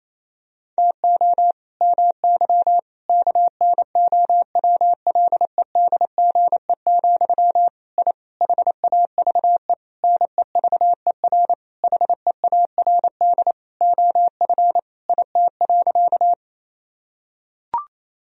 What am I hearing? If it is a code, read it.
Morse code, heard as a keyed tone, about 28 wpm, 700 Hz. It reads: TO MY KNOWLEDGE, I HAVE NEVER HEARD OF IT.
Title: Sentences from Top 1000 Words - Review Part 3 - 28wpm